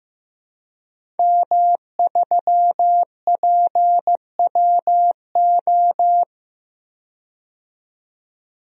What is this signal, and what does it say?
Morse code, heard as a keyed tone, about 15 wpm, 700 Hz. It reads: M3PWO
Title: DXCC-100 Call Signs - 5 Characters - 15wpm